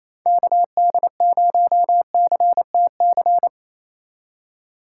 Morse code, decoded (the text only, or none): KD0CTC